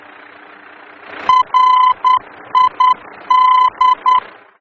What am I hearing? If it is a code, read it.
Morse code, heard as a keyed tone, about 10 wpm, 1010 Hz. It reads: RID
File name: germany / mx / RID-D-VOR